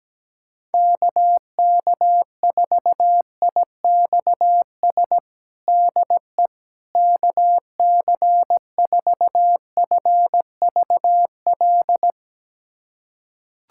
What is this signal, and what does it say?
Morse code, heard as a keyed tone, about 17 wpm, 700 Hz. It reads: KK4IXS DE KC4FVL
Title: USA Callsign DE Callsign - 6 Characters - 17wpm